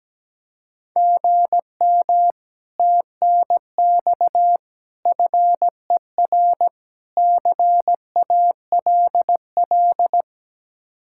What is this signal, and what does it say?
Morse code, heard as a keyed tone, about 17 wpm, 700 Hz. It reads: GM TNX FER CALL
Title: Speed-Racing Common QSO Phrases - 17wpm